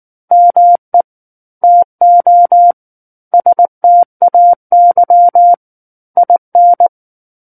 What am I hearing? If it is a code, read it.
Morse code, heard as a keyed tone, about 19 wpm, 700 Hz. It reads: ME TO STAY IN
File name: Book One / 10-Chapter 10 - 19 WPM